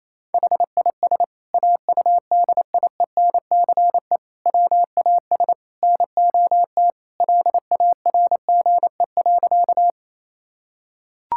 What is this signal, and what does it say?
Morse code, heard as a keyed tone, about 28 wpm, 700 Hz. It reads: HIS AUDIENCE WAS NOT LARGE.
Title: Sentences from Top 1000 Words - Review Part 5 - 28wpm